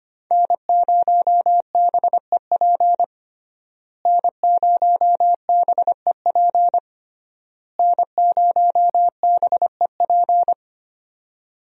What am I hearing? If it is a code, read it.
Morse code, heard as a keyed tone, about 25 wpm, 700 Hz. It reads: N0BEP N0BEP N0BEP
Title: USA Call Signs - 5 Characters - Repeated 3x - 25wpm